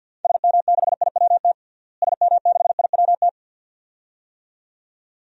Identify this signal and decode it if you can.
Morse code, heard as a keyed tone, about 53 wpm, 700 Hz. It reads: SM6IWT SM6IWT
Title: DXCC-100 Call Signs - 6 Characters - Repeated 2x - 50wpm